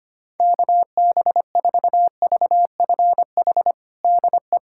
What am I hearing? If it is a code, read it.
Morse code, heard as a keyed tone, about 25 wpm, 700 Hz. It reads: KB4VFH DE
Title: USA Callsign DE Callsign 25wpm